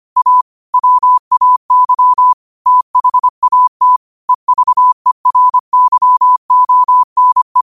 Morse code, heard as a keyed tone, about 25 wpm, 1000 Hz. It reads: A WAY THAT EVERYONE
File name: qotdfiles / 25wpm